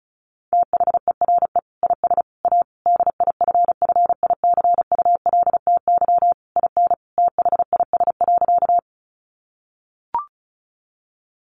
Morse code, heard as a keyed tone, about 34 wpm, 700 Hz. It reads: THERE IS A DIFFICULTY IN THIS.
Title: Sentences from Top 1500 Words 35wpm